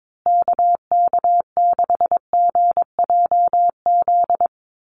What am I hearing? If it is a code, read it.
Morse code, heard as a keyed tone, about 22 wpm, 700 Hz. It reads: KK6GJZ
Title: USA Call Signs Repeated 3x 22wpm